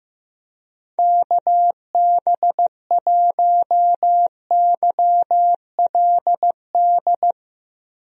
Morse code, heard as a keyed tone, about 15 wpm, 700 Hz. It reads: KB1YLD
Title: USA Call Signs - 6 Characters - Repeated 2x - 15wpm